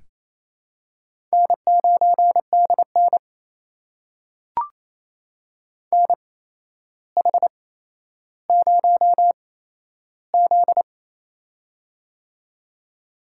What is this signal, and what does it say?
Morse code, heard as a keyed tone, about 28 wpm, spaced as for 12 wpm, 700 Hz. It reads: N9DN NH0Z
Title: USA Call Signs - 4 Characters - 8x Character Spacing - 28wpm